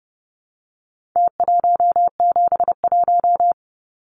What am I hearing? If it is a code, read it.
Morse code, heard as a keyed tone, about 30 wpm, 700 Hz. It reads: T171
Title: Numbers - Contest Style - 4-Digit 30wpm